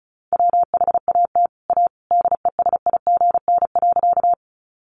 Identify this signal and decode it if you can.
Morse code, heard as a keyed tone, about 34 wpm, 700 Hz. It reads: WHAT A DESIGN.
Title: Sentences from Top 800 Words 35wpm